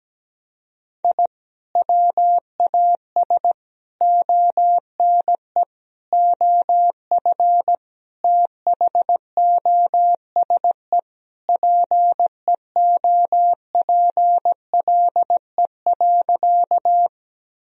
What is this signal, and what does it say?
Morse code, heard as a keyed tone, about 17 wpm, 700 Hz. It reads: I WAS ONE OF THOSE PEOPLE.
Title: Sentences from Top 1000 Words - Review Part 1 - 17wpm